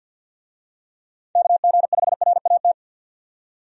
Morse code, heard as a keyed tone, about 53 wpm, 700 Hz. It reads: KG5RAT
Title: USA Call Signs - 6 Characters - 50wpm